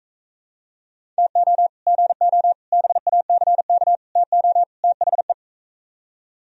Morse code, heard as a keyed tone, about 44 wpm, 700 Hz. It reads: TO GO BACK TO THE